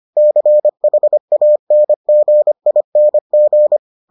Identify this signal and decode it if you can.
Morse code, heard as a keyed tone, about 25 wpm, 600 Hz. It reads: CHANGING